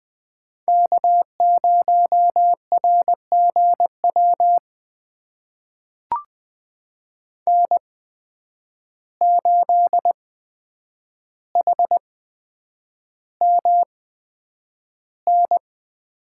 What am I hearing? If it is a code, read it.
Morse code, heard as a keyed tone, about 20 wpm, spaced as for 9 wpm, 700 Hz. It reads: K0RGW N8HMN
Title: USA Call Signs - 5 Characters - 8x Character Spacing - 20wpm